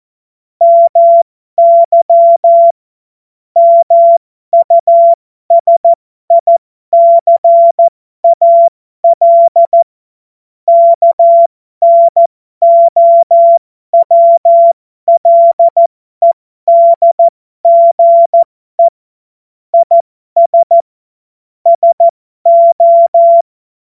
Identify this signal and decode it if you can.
Morse code, heard as a keyed tone, about 14 wpm, spaced as for 12 wpm, 680 Hz. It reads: MY MUSICAL KNOWLEDGE IS SO
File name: Morsekurs / Witze